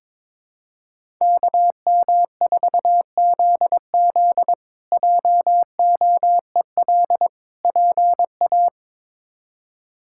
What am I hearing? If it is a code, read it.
Morse code, heard as a keyed tone, about 21 wpm, 700 Hz. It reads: KM4ZZ JOEL PA
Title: K1USN SST QSO Practice - 22wpm